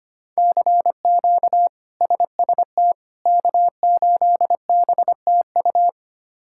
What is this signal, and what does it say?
Morse code, heard as a keyed tone, about 25 wpm, 700 Hz. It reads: CQ SST K8BTU
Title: K1USN SST QSO Practice - 25wpm